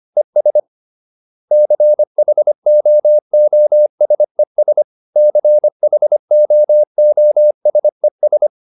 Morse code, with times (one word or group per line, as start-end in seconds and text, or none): 0.2-0.6 ES
1.5-4.8 CHOOSES
5.2-8.5 CHOOSES